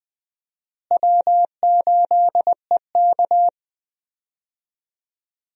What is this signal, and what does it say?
Morse code, heard as a keyed tone, about 20 wpm, 700 Hz. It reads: W8EK
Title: USA Call Signs - 4 Characters - Repeated 2x - 20wpm